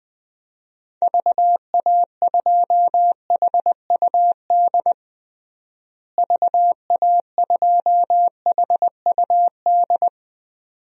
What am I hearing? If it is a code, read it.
Morse code, heard as a keyed tone, about 20 wpm, 700 Hz. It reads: VA2HUD VA2HUD
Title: DXCC-100 Call Signs - 6 Characters - Repeated 2x - 20wpm